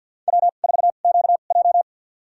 Morse code, heard as a keyed tone, about 48 wpm, spaced as for 35 wpm, 700 Hz. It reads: W4QJ
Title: USA Call Signs - 4 Characters - 2x Character Spacing - 50wpm